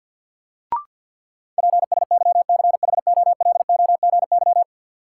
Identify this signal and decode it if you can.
Morse code, heard as a keyed tone, about 53 wpm, 700 Hz. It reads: PSYCHOLOGY
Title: Sentences from Top 1400 Words - New Word Drill - 50wpm